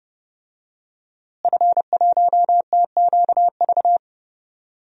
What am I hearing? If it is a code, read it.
Morse code, heard as a keyed tone, about 30 wpm, 700 Hz. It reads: F1TQV
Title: DXCC-100 Call Signs - 5 Characters - Repeated 2x - 30wpm